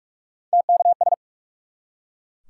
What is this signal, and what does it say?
Morse code, heard as a keyed tone, about 44 wpm, 700 Hz. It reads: TKS
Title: Top 180 QSO Elements 45wpm